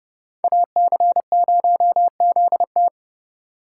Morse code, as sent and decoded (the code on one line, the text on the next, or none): .- -.-. ----- --.. -
AC0ZT